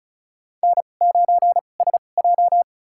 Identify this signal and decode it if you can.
Morse code, heard as a keyed tone, about 36 wpm, spaced as for 25 wpm, 700 Hz. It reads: N9SJ
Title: USA Call Signs - 4 Characters - 2x Character Spacing - 35wpm